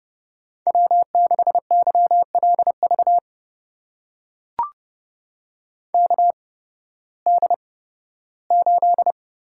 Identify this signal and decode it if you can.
Morse code, heard as a keyed tone, about 30 wpm, spaced as for 16 wpm, 700 Hz. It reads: W6YLV KD8
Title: USA Call Signs - 8x Character Spacing - 30wpm